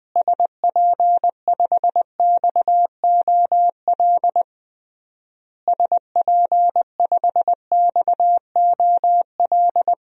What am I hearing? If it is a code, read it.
Morse code, heard as a keyed tone, about 20 wpm, 700 Hz. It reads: SP5XOL SP5XOL
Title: DXCC-100 Call Signs - 6 Characters - Repeated 3x - 20wpm